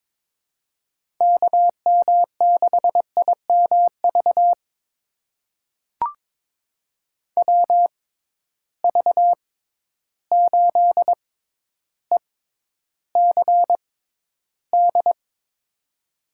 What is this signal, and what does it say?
Morse code, heard as a keyed tone, about 22 wpm, spaced as for 11 wpm, 700 Hz. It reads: KM6IMV WV8ECD
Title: USA Call Signs - 6 Characters - 6x Character Spacing - 22wpm